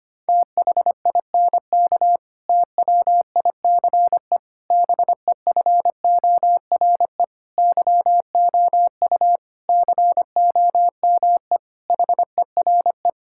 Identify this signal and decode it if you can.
Morse code, heard as a keyed tone, about 25 wpm, 700 Hz. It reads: THINK TWICE BEFORE YOU COME HERE
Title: Sentences from Top 1600 Words 25wpm